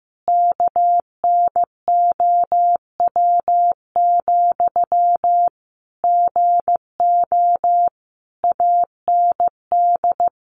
Morse code, heard as a keyed tone, about 15 wpm, 700 Hz. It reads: KNOW, GO AND
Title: Sentences from Top 200 Words 15wpm